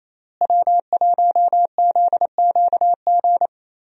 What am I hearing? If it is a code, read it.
Morse code, heard as a keyed tone, about 27 wpm, 700 Hz. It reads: W1ZQG
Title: USA Call Signs - 5 Characters - Repeated 3x - 28wpm